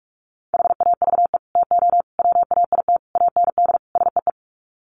Morse code, heard as a keyed tone, about 44 wpm, 700 Hz. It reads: HAVE TO WAIT AND SEE
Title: Sets of 5 Words Encore2 45wpm